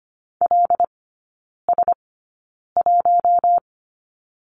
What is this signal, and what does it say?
Morse code, heard as a keyed tone, about 25 wpm, spaced as for 9 wpm, 700 Hz. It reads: LS1
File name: Lesson 15e - Number 5 - Three Characters 25wpm